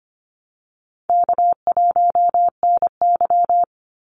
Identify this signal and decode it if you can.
Morse code, heard as a keyed tone, about 25 wpm, 700 Hz. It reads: K1NY
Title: Easy Call Signs Repeated 2x - 25wpm